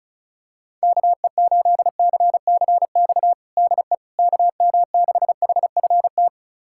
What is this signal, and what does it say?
Morse code, heard as a keyed tone, about 36 wpm, 700 Hz. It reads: KE8CCX DE KM6HFT
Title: USA Callsign DE Callsign - 6 Characters - 35wpm